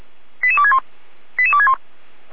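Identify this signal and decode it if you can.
Five-tone selective call (ZVEI), heard as quick runs of five tones, tones of about 70 ms. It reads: 88261 88261
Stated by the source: Recorded via SDR by feuerwehr-orlamuende.org (fire brigade Orlamünde), ZVEI five-tone call 88335.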